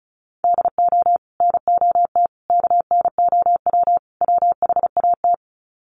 Morse code, heard as a keyed tone, about 34 wpm, 700 Hz. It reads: DO NOT KNOW WHAT
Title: Sets of 4 Words 35wpm